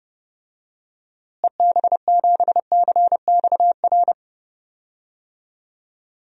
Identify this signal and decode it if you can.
Morse code, heard as a keyed tone, about 32 wpm, 700 Hz. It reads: EB7CXR
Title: DXCC-100 Call Signs - 6 Characters - Repeated 3x - 30wpm